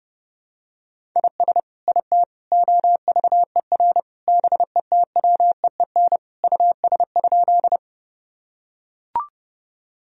Morse code, heard as a keyed tone, about 30 wpm, 700 Hz. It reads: IS IT OVER BETWEEN US?
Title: Sentences from Top 1000 Words - Review Part 1 - 30wpm